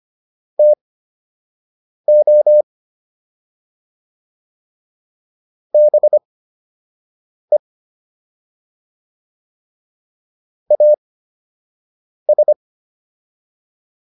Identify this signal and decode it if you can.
Morse code, heard as a keyed tone, about 25 wpm, spaced as for 6 wpm, 600 Hz. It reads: TO BE AS